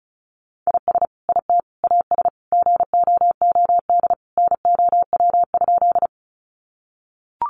Morse code, heard as a keyed tone, about 34 wpm, 700 Hz. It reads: IS IT AS GOOD NOW?
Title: Sentences from Top 200 Words 35wpm